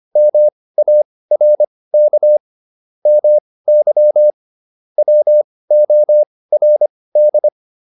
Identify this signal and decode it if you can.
Morse code, heard as a keyed tone, about 25 wpm, spaced as for 18 wpm, 600 Hz. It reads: MARK MY WORD